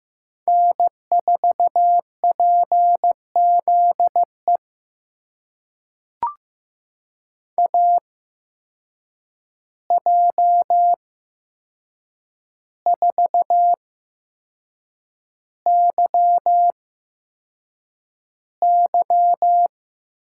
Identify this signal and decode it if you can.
Morse code, heard as a keyed tone, about 15 wpm, spaced as for 6 wpm, 700 Hz. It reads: N4PZE AJ4YY
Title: USA Call Signs - 8x Character Spacing - 15wpm